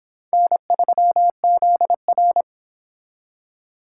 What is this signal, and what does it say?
Morse code, heard as a keyed tone, about 24 wpm, 700 Hz. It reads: N3ZR